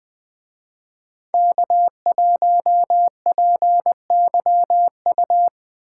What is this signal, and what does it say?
Morse code, heard as a keyed tone, about 20 wpm, 700 Hz. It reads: K1PYU
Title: Speed Racing - USA 1x3 Call Signs 17wpm